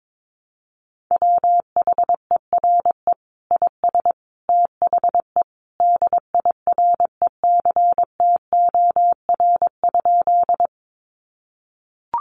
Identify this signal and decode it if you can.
Morse code, heard as a keyed tone, about 22 wpm, 700 Hz. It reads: WHERE IS THE DIRECTOR?
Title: Sentences from Top 1300 Words 22wpm